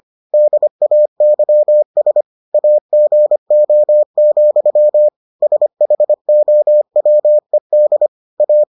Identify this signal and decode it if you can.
Morse code, heard as a keyed tone, about 25 wpm, 600 Hz. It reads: DAYS AGO, SHOWED A